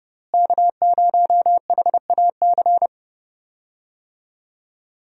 Morse code, heard as a keyed tone, about 31 wpm, 700 Hz. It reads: K0HAC
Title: USA Call Signs - 5 Characters - Repeated 3x - 30wpm